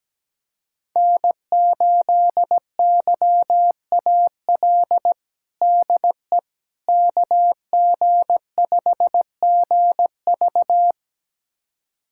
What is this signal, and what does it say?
Morse code, heard as a keyed tone, about 17 wpm, 700 Hz. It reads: N8YAL DE KG5GV
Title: USA Callsign DE Callsign - 5 Characters - 17wpm